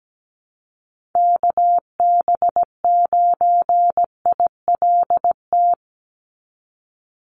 Morse code, heard as a keyed tone, about 17 wpm, 700 Hz. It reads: KB9ILT
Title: NA SOTA Activator Callsigns 17wpm